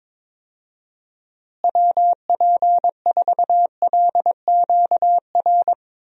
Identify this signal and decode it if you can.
Morse code, heard as a keyed tone, about 21 wpm, 700 Hz. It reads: WP4LQR
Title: USA Call Signs - 6 Characters - Repeated 3x - 22wpm